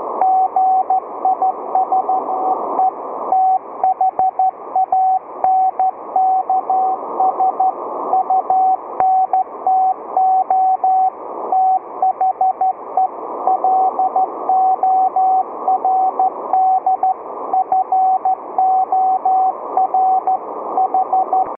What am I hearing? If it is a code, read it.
Morse code, heard as a keyed tone, about 14 wpm, 760 Hz, with radio noise. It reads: GI SEE THANKS UNTO THE LORD FOR H